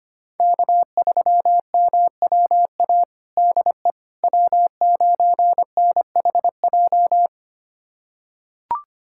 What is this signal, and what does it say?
Morse code, heard as a keyed tone, about 25 wpm, 700 Hz. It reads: K3MWA DE W9NHJ
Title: USA Callsign DE Callsign - 5 Characters - 25wpm